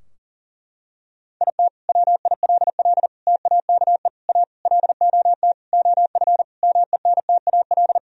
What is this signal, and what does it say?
Morse code, heard as a keyed tone, about 40 wpm, 700 Hz. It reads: IT WILL TAKE A LOT OF MENTAL
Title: Sentences from Top 1000 Words - Review Part 4 - 40wpm